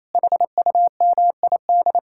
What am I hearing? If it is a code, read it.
Morse code, heard as a keyed tone, about 34 wpm, 700 Hz. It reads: HUMID
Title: Speed-Racing Common QSO Phrases Encore - 28wpm